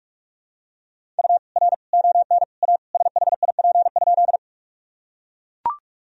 Morse code, heard as a keyed tone, about 41 wpm, 700 Hz. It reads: U R ON A SHIP?
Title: Sentences from Top 400 Words and Abbreviations 45wpm